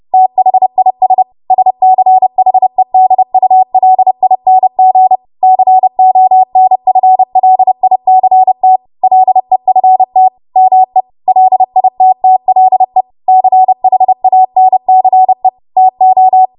Morse code, heard as a keyed tone, about 30 wpm, 750 Hz. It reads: THIS SCHEDULING CONFLICT LEFT ME LITTLE CHANCE TO
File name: Morse / 30WPM